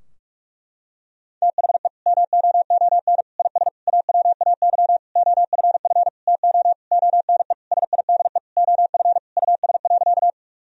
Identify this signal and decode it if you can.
Morse code, heard as a keyed tone, about 41 wpm, 700 Hz. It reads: THE MOON IS AWAY OFF TO ONE SIDE OF US.
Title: Sentences from Top 1000 Words - Review Part 3 - 45wpm